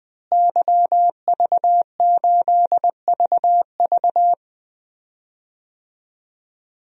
Morse code, heard as a keyed tone, about 20 wpm, 700 Hz. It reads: YV8VV
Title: DXCC-100 Call Signs - 5 Characters - Repeated 2x - 20wpm